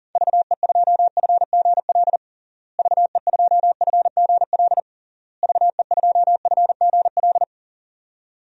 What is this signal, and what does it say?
Morse code, heard as a keyed tone, about 38 wpm, 700 Hz. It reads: VE2FGL VE2FGL VE2FGL
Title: DXCC-100 Call Signs - 6 Characters - Repeated 3x - 40wpm